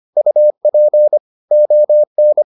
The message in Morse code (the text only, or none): UP ON